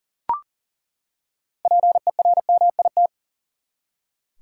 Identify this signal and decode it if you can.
Morse code, heard as a keyed tone, about 40 wpm, 700 Hz. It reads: PERMIT
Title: Six Letter Words - 40wpm